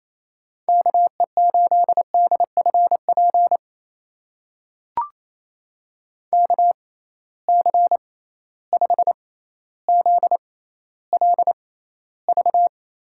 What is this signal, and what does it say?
Morse code, heard as a keyed tone, about 28 wpm, spaced as for 14 wpm, 700 Hz. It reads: KE8DFP KC5ZLV
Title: USA Call Signs - 6 Characters - 6x Character Spacing - 28wpm